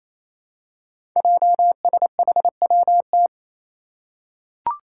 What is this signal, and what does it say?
Morse code, heard as a keyed tone, about 29 wpm, 700 Hz. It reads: JSHWT
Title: Five Challenging Letter-Number Combinations 28wpm